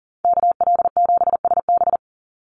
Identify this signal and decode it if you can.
Morse code, heard as a keyed tone, about 40 wpm, 700 Hz. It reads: KL7SB
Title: Call Signs - 40wpm